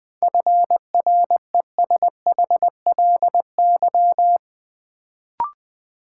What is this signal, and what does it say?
Morse code, heard as a keyed tone, about 20 wpm, 700 Hz. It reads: FRESHLY
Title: Sets of 5 Words Encore - New Word Drill - 20wpm